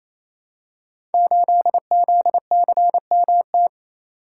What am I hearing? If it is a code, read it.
Morse code, heard as a keyed tone, about 28 wpm, 700 Hz. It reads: 8ZCMT
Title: Code Groups - 5 Letter:Numbers - 28wpm